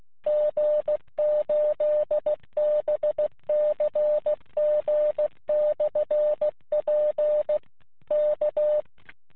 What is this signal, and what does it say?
Morse code, heard as a keyed tone, about 15 wpm, 600 Hz. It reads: G8BCG/P K
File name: jh5fis-qso